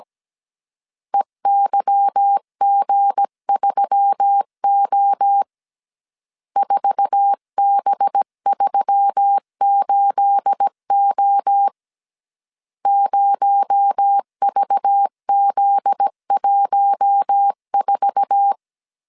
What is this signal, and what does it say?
Morse code, heard as a keyed tone, about 17 wpm, 800 Hz. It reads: EYG3O 4B38O 0VZ14
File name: 1500C S B 16wpm